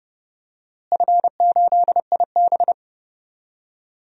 Morse code, heard as a keyed tone, about 30 wpm, 700 Hz. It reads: F8IB
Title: DXCC-100 Call Signs - 4 Characters - Repeated 3x - 30wpm